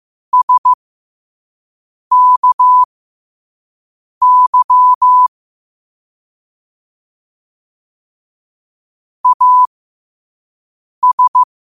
Morse code, heard as a keyed tone, about 15 wpm, spaced as for 5 wpm, 1000 Hz. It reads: SKY AS
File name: qotdfiles / 5wpm